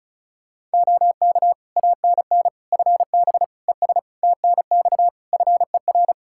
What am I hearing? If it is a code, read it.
Morse code, heard as a keyed tone, about 34 wpm, 700 Hz. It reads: OK ANN FB ES TNX FER
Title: K1USN SST QSO Practice 35wpm